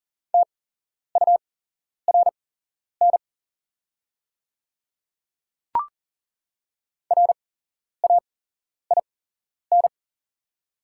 Morse code, heard as a keyed tone, about 40 wpm, spaced as for 11 wpm, 690 Hz. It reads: TURN RAIN
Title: 8x Character Spacing - Top 300 Words - 40wpm